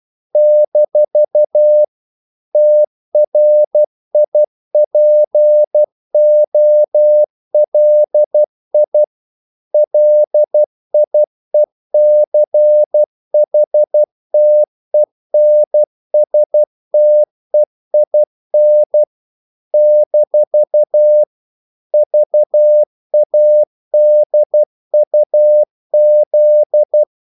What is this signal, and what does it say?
Morse code, heard as a keyed tone, about 12 wpm, 600 Hz. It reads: - TRIPOLI LIECHTENSTEIN - VADUZ
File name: CW / Practice / NationalCapitals12